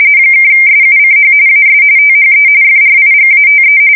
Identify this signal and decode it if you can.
RTTY, about 45.45 baud. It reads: (SUCH AS THE FRIGATE
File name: fsk170 / USS-New-Jersey